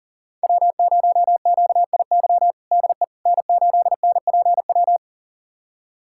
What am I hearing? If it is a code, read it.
Morse code, heard as a keyed tone, about 37 wpm, 700 Hz. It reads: W0QIY DE N8NPW
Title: USA Callsign DE Callsign - 5 Characters - 40wpm